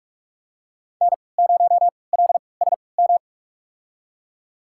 Morse code, heard as a keyed tone, about 43 wpm, spaced as for 25 wpm, 700 Hz. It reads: N0LSM
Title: USA Call Signs - 5 Characters - 3x Character Spacing - 45wpm